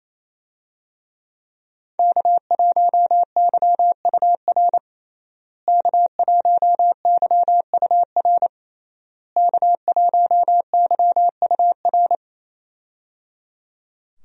Morse code, heard as a keyed tone, about 28 wpm, 700 Hz. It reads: K1YUR K1YUR K1YUR
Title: USA Call Signs - 5 Characters - Repeated 3x - 28wpm